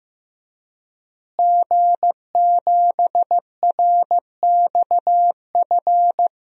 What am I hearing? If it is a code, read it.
Morse code, heard as a keyed tone, about 15 wpm, 700 Hz. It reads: G7RXF
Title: DXCC-100 Call Signs - 5 Characters - Repeated 2x - 15wpm